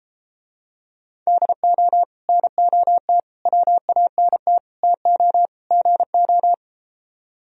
Morse code, heard as a keyed tone, about 33 wpm, 700 Hz. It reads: DO NOT WANT TO GO